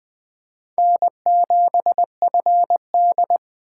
Morse code, heard as a keyed tone, about 24 wpm, 700 Hz. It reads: N7FD
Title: Speed Racing - USA 1x2 Call Signs 20wpm